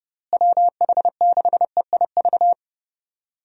Morse code, heard as a keyed tone, about 32 wpm, 700 Hz. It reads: WH6EIV
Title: USA Call Signs - 6 Characters - Repeated 3x - 30wpm